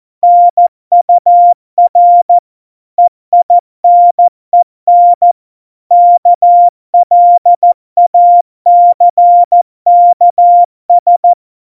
Morse code, heard as a keyed tone, about 14 wpm, 700 Hz. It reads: NUR EINEN KLACKS
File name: Morsekurs / Witze / Witz 03 April 2018, 70 F80